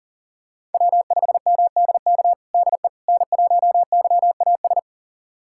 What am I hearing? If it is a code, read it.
Morse code, heard as a keyed tone, about 40 wpm, 700 Hz. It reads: W5MDK DE N1YAS
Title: USA Callsign DE Callsign - 5 Characters - 40wpm